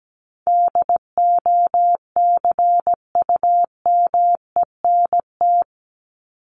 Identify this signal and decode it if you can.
Morse code, heard as a keyed tone, about 17 wpm, 700 Hz. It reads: DOCUMENT
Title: Top 1751-2000 Words 17wpm